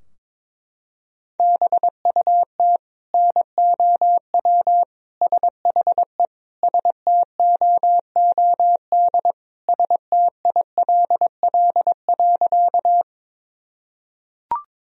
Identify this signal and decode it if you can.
Morse code, heard as a keyed tone, about 22 wpm, 700 Hz. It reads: BUT NOW SHE STOOD STILL.
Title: Sentences from Top 1000 Words - Review Part 2 - 22wpm